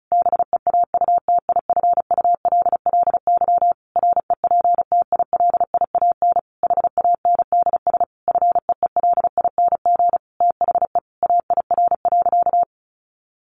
Morse code, heard as a keyed tone, about 34 wpm, 700 Hz. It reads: BEAUTIFULLY REPTILIAN HANDS FEELING THE AIR.
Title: A Sound Of Thunder 35wpm